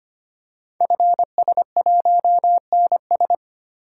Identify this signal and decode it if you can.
Morse code, heard as a keyed tone, about 25 wpm, 700 Hz. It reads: FS1NS
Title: Code Groups - 5 Letter:Numbers - 25wpm